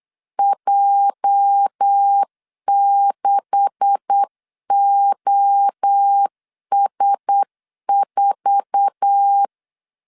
Morse code, heard as a keyed tone, about 8 wpm, 800 Hz. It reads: J6OS4
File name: CW / DET / 1 9 1500C S A 8wpm 5637